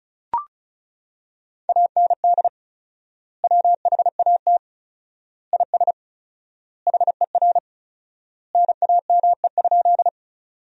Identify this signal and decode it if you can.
Morse code, heard as a keyed tone, about 35 wpm, 700 Hz. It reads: AND WHAT IS HER NAME?
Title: 4x Word-Spacing - Sentences from Top 200 Words - 35wpm